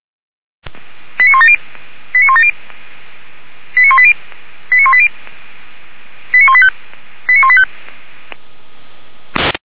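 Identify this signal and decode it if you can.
Five-tone selective call (ZVEI), heard as quick runs of five tones, tones of about 80 ms. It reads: 89170 89170 89180 89180 89185 89185
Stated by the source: Five-tone ZVEI selective call received by SDR, from a fire brigade (Feuerwehr) alerting archive.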